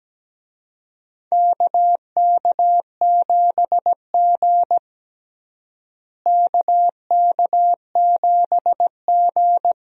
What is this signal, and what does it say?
Morse code, heard as a keyed tone, about 17 wpm, 700 Hz. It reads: KK7G KK7G
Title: USA Call Signs - 4 Characters - Repeated 3x - 17wpm